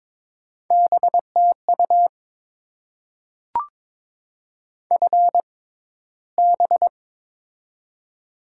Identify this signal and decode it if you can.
Morse code, heard as a keyed tone, about 22 wpm, spaced as for 14 wpm, 690 Hz. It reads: BTU FB
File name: Top 20 QSO Elements - 6x Character Spacing - 22wpm